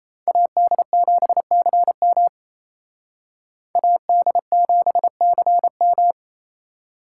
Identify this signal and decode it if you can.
Morse code, heard as a keyed tone, about 35 wpm, 700 Hz. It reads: AD7CM AD7CM